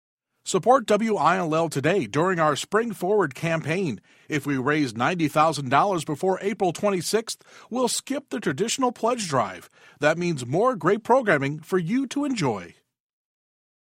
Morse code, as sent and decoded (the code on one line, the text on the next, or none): none